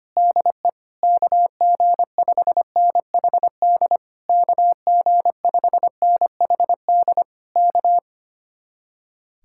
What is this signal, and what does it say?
Morse code, heard as a keyed tone, about 24 wpm, 700 Hz. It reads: DE KG5NHD KG5NHD K
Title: Speed-Racing Common QSO Phrases - 25wpm